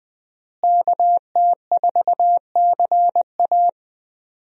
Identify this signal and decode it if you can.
Morse code, heard as a keyed tone, about 20 wpm, 700 Hz. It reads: KT4CA
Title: USA Call Signs - 5 Characters - Repeated 2x - 20wpm